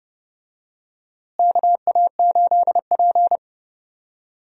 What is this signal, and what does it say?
Morse code, heard as a keyed tone, about 32 wpm, 700 Hz. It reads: KA8P
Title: USA Call Signs - 4 Characters - Repeated 2x - 30wpm